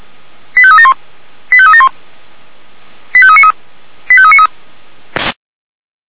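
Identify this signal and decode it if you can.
Five-tone selective call (ZVEI), heard as quick runs of five tones, tones of about 70 ms. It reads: 85371 85371 85393 85393
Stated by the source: ZVEI five-tone selective call received by SDR.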